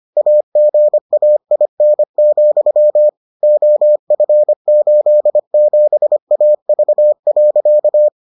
Morse code, heard as a keyed tone, about 25 wpm, 600 Hz. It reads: AGAIN, OF87AV.